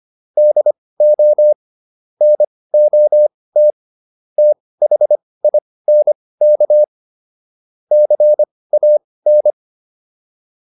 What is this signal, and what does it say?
Morse code, heard as a keyed tone, about 25 wpm, spaced as for 18 wpm, 600 Hz. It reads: DO NOT THINK CAN